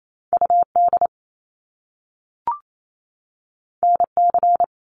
Morse code, heard as a keyed tone, about 28 wpm, 700 Hz. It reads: UD NC
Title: Top Two Letter Combinations in Words 28wpm